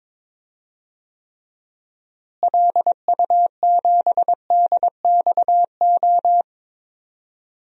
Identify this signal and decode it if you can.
Morse code, heard as a keyed tone, about 22 wpm, 700 Hz. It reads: LU7DXO
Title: DXCC-100 Call Signs - 6 Characters - Repeated 2x - 22wpm